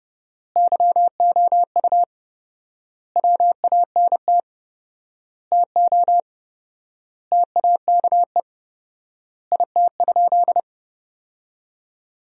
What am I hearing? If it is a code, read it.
Morse code, heard as a keyed tone, about 30 wpm, 700 Hz. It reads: YOU WANT TO TAKE IT?
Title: 4x Word-Spacing - Sentences from Top 300 Words - 30wpm